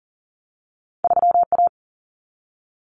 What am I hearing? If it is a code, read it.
Morse code, heard as a keyed tone, about 40 wpm, 700 Hz. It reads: 3A
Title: Common Cut Numbers - 2-Digit 40wpm